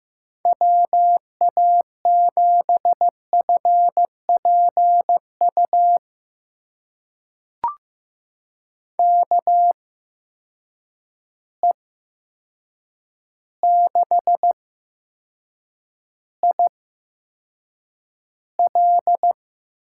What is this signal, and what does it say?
Morse code, heard as a keyed tone, about 15 wpm, spaced as for 7 wpm, 700 Hz. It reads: WA7FPU KE6IL
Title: USA Call Signs - 6 Characters - 8x Character Spacing - 15wpm